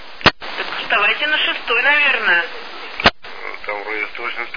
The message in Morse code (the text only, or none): none